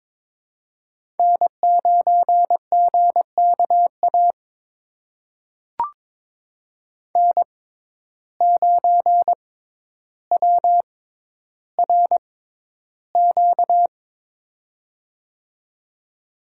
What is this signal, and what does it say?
Morse code, heard as a keyed tone, about 22 wpm, spaced as for 11 wpm, 700 Hz. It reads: N9GKA N9WRQ
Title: USA Call Signs - 5 Characters - 6x Character Spacing - 22wpm